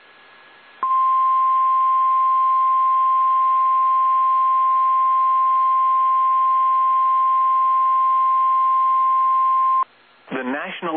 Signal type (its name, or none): Emergency Alert System (SAME) tones